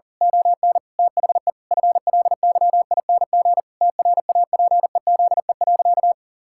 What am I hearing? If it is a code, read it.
Morse code, heard as a keyed tone, about 40 wpm, 700 Hz. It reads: ON THE FLYING TRAPEZE.
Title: Pangram Challenge v2 - 40wpm